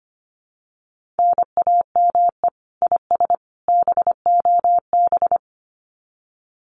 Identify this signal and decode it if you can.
Morse code, heard as a keyed tone, about 25 wpm, 700 Hz. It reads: NAME IS BOB
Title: Common QSO Phrases 25wpm